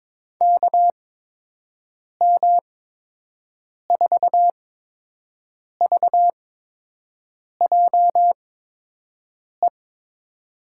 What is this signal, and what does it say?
Morse code, heard as a keyed tone, about 23 wpm, spaced as for 6 wpm, 700 Hz. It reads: KM4VJE
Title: USA Call Signs - 6 Characters - 8x Character Spacing - 22wpm